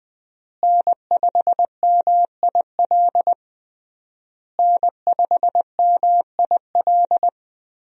Morse code, heard as a keyed tone, about 21 wpm, 700 Hz. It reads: N5MIL N5MIL
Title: USA Call Signs - 5 Characters - Repeated 3x - 20wpm